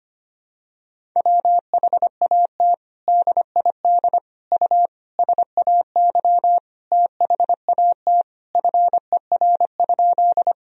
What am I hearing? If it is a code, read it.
Morse code, heard as a keyed tone, about 25 wpm, 700 Hz. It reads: WHAT DID U SAY THAT FER?
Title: Sentences from Top 200 Words and Abbreviations 25wpm